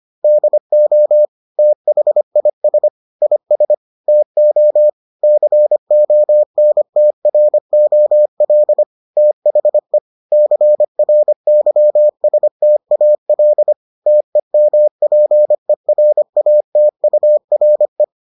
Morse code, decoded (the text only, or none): DO THIS IS TO CONTROL THE CRYSTAL TEMPERATURE